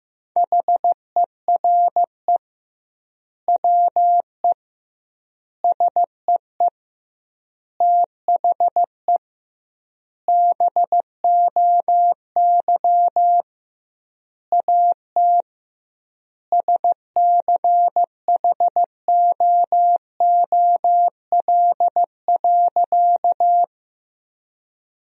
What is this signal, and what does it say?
Morse code, heard as a keyed tone, about 15 wpm, 700 Hz. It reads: HERE WE SEE THE BOY AT SCHOOL.
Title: 2x Word-Spacing - Sentences from Top 300 Words - 15wpm